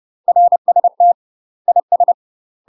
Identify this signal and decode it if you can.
Morse code, heard as a keyed tone, about 30 wpm, 700 Hz. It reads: RST IS
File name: UK MASTER QSOs TRAINING/30WPM